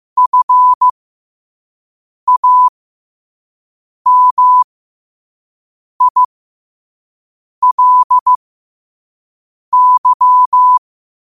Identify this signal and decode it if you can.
Morse code, heard as a keyed tone, about 15 wpm, spaced as for 5 wpm, 1000 Hz. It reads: FAMILY
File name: qotdfiles / 5wpm